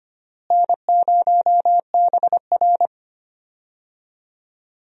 Morse code, heard as a keyed tone, about 25 wpm, 700 Hz. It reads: N0BR
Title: USA Call Signs - 4 Characters - Repeated 3x - 25wpm